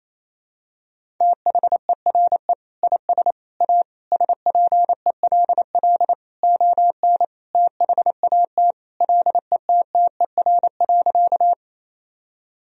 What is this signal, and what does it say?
Morse code, heard as a keyed tone, about 28 wpm, 700 Hz. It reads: THERE IS A SPELL ON THAT LETTER.
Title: Sentences from Top 1000 Words - Review Part 1 - 28wpm